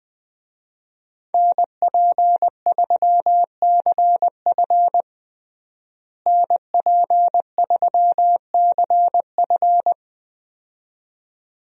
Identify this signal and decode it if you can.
Morse code, heard as a keyed tone, about 21 wpm, 700 Hz. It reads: NP3CF NP3CF
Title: USA Call Signs - 5 Characters - Repeated 2x - 20wpm